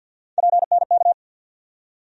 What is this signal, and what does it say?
Morse code, heard as a keyed tone, about 51 wpm, 690 Hz. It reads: PNK
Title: USA Call Sign Postfixes - 3 Letters 50wpm